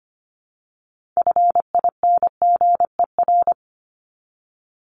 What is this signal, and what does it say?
Morse code, heard as a keyed tone, about 25 wpm, 700 Hz. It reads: FINGER
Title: Top 1501-1750 Words 25wpm